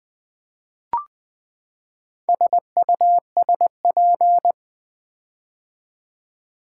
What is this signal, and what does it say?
Morse code, heard as a keyed tone, about 20 wpm, 700 Hz. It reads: SUSP
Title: Build A Word - 4 Letters At A Time - 10-Letter Words - 20wpm